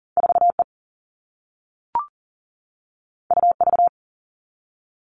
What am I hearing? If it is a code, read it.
Morse code, heard as a keyed tone, about 39 wpm, 700 Hz. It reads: 4E UV
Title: All Cut Numbers - 2-Digit 40wpm